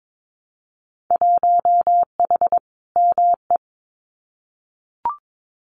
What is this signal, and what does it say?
Morse code, heard as a keyed tone, about 22 wpm, 700 Hz. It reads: 1H ME
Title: Winter Field Day Exchanges 22wpm